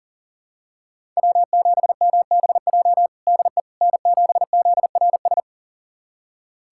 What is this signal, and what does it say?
Morse code, heard as a keyed tone, about 40 wpm, 700 Hz. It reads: W7MBJ DE N7ZRS
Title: USA Callsign DE Callsign - 5 Characters - 40wpm